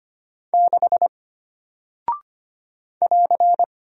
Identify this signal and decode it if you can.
Morse code, heard as a keyed tone, about 25 wpm, 700 Hz. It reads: 6 <AR>
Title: Single Character (Letter:Number:Punctuation:Prosign) Alt 25wpm